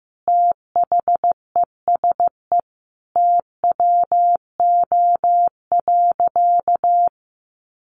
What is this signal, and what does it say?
Morse code, heard as a keyed tone, about 15 wpm, 700 Hz. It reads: THESE TWO.
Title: Sentences from Top 1200 Words 15wpm